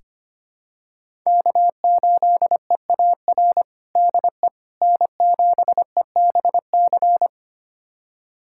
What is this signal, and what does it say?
Morse code, heard as a keyed tone, about 25 wpm, 700 Hz. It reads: K8EAR DE N7EBC
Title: USA Callsign DE Callsign - 5 Characters - 25wpm